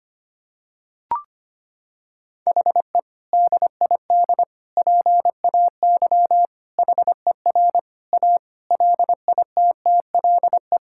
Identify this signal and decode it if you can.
Morse code, heard as a keyed tone, about 25 wpm, 700 Hz. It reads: HE DID PAY HER A LITTLE
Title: Sentences from Top 1000 Words - Review Part 6 - 25wpm